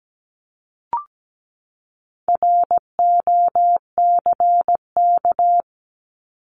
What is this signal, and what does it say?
Morse code, heard as a keyed tone, about 17 wpm, 700 Hz. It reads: ROCK
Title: Top 500 Words 17wpm